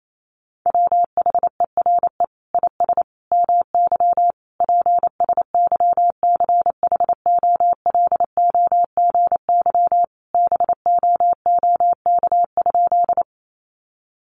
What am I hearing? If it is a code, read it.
Morse code, heard as a keyed tone, about 28 wpm, 700 Hz. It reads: WHERE IS MY PSYCHOLOGY BOOK?
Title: Sentences from Top 1400 Words 28wpm